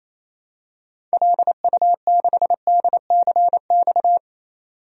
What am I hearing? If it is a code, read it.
Morse code, heard as a keyed tone, about 27 wpm, 700 Hz. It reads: LU6DCX
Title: DXCC-100 Call Signs - 6 Characters - 28wpm